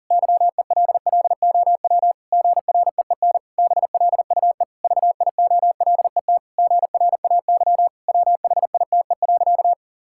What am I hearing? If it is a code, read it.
Morse code, heard as a keyed tone, about 37 wpm, 700 Hz. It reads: YELLOW GREEN BLUE VIOLET GRAY WHITE.
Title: Pangram Challenge v2 - 40wpm